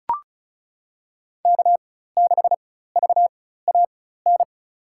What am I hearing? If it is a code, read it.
Morse code, heard as a keyed tone, about 36 wpm, spaced as for 16 wpm, 700 Hz. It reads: K6VAN
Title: USA Call Signs - 5 Characters - 4x Character Spacing - 35wpm